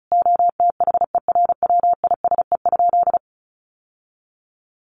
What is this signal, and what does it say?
Morse code, heard as a keyed tone, about 34 wpm, 700 Hz. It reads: OTHERWISE?
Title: Sentences from Top 1200 Words 35wpm